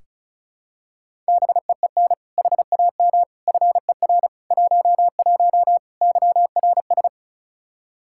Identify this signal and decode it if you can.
Morse code, heard as a keyed tone, about 36 wpm, 700 Hz. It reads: BEEN HAM FER 11 YRS
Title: Practice QSO - Exchange 3 - More Info - 35wpm